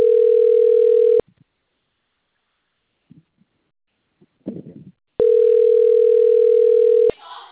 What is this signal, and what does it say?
Ringback tone (North America), 440 + 480 Hz, with speech.
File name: out- - - - - .12871